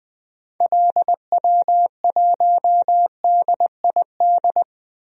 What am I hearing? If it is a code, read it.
Morse code, heard as a keyed tone, about 20 wpm, 700 Hz. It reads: LW1DID
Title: DXCC-100 Call Signs - 6 Characters - 20wpm